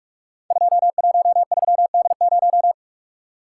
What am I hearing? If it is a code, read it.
Morse code, heard as a keyed tone, about 44 wpm, 700 Hz. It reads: 213D0
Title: Code Groups - 5 Letter:Numbers - 45wpm